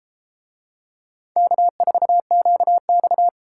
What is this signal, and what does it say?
Morse code, heard as a keyed tone, about 32 wpm, 700 Hz. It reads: K4QX